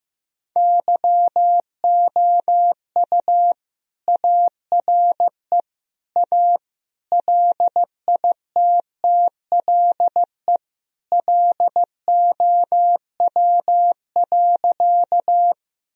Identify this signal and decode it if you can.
Morse code, heard as a keyed tone, about 16 wpm, 700 Hz. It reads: YOU ARE A LITTLE LOW.
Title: Sentences from Top 1000 Words - Review Part 1 - 15wpm